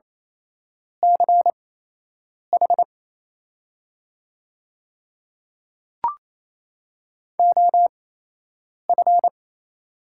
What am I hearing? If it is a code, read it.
Morse code, heard as a keyed tone, about 28 wpm, spaced as for 8 wpm, 690 Hz. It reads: CH OF
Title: 8x Character Spacing - Top 100 Words - 28wpm